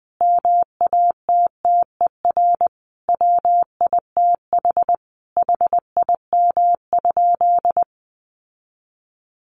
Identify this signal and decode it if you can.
Morse code, heard as a keyed tone, about 20 wpm, 700 Hz. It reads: MATTER WITH HIM?
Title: Sentences from Top 700 Words 20wpm